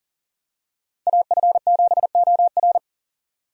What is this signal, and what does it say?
Morse code, heard as a keyed tone, about 40 wpm, 700 Hz. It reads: AF7OR
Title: USA Call Signs - 5 Characters - 40wpm